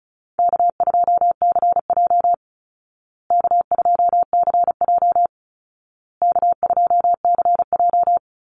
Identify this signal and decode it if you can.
Morse code, heard as a keyed tone, about 34 wpm, 700 Hz. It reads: K2CJ K2CJ K2CJ
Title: Call Signs Repeated 2x - 35wpm